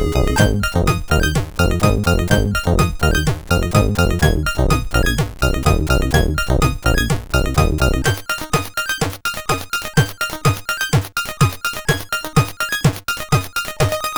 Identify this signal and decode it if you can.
There is no code.